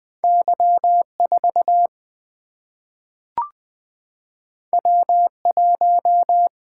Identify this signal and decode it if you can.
Morse code, heard as a keyed tone, about 20 wpm, 700 Hz. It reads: Y 4 W 1
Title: Single-Letter Single-Number 20wpm